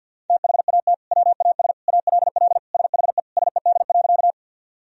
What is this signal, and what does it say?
Morse code, heard as a keyed tone, about 50 wpm, 700 Hz. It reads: THAT WAS ALL SHE SED.
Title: Sentences from Top 100 Words and Abbreviations 50wpm